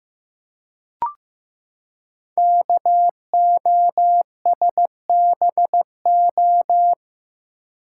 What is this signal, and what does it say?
Morse code, heard as a keyed tone, about 15 wpm, 700 Hz. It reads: KOSBO
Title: Code Groups - 5 Letters - 15wpm